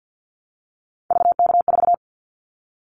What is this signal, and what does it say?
Morse code, heard as a keyed tone, about 50 wpm, 700 Hz. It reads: VK4
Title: Callsign Prefixes for Top 50 DXCC - 50wpm